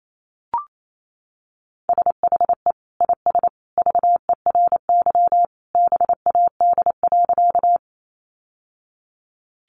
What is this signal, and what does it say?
Morse code, heard as a keyed tone, about 28 wpm, 700 Hz. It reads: SHE IS VERY BAD.
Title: Sentences from Top 700 Words 28wpm